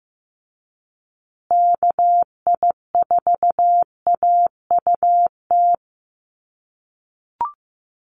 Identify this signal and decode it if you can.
Morse code, heard as a keyed tone, about 15 wpm, 700 Hz. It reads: KI4AUT
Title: USA Call Signs Repeated 2x 15wpm